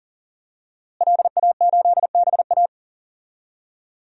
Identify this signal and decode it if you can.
Morse code, heard as a keyed tone, about 40 wpm, 700 Hz. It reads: LA8BA
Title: DXCC-100 Call Signs - 5 Characters - Repeated 2x - 40wpm